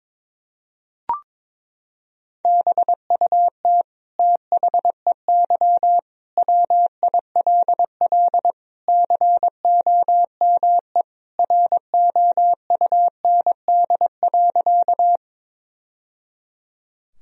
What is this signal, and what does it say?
Morse code, heard as a keyed tone, about 22 wpm, 700 Hz. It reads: BUT THEY WILL COME ROUND.
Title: Sentences from Top 1000 Words - Review Part 1 - 22wpm